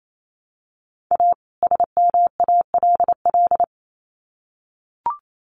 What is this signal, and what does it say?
Morse code, heard as a keyed tone, about 28 wpm, 700 Hz. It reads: A SMALL
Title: Sets of 2 Words 28wpm